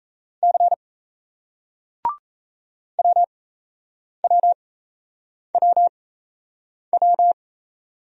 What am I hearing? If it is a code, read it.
Morse code, heard as a keyed tone, about 36 wpm, spaced as for 8 wpm, 700 Hz. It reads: C WWWW